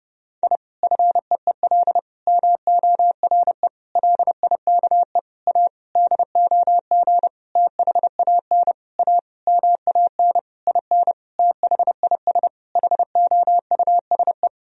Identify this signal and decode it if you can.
Morse code, heard as a keyed tone, about 30 wpm, 700 Hz. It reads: I FEEL MORE LIKE A DOG THAN A MAN IN THIS HOUSE
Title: Sentences from Top 1000 Words - Review Part 2 - 30wpm